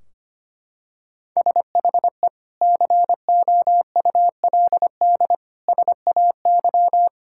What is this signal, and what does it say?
Morse code, heard as a keyed tone, about 25 wpm, 700 Hz. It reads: SHE COULD SAY
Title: Sentences from Top 1600 Words 25wpm